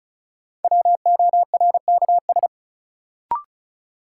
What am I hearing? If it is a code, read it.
Morse code, heard as a keyed tone, about 36 wpm, 700 Hz. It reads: WORKS
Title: Sentences from Top 1000 Words - Review Part 5 - New Word Drill - 35wpm